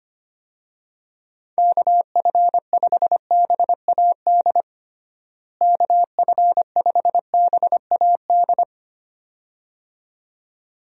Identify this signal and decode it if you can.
Morse code, heard as a keyed tone, about 25 wpm, 700 Hz. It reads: KF5BAD KF5BAD
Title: USA Call Signs - 6 Characters - Repeated 2x - 25wpm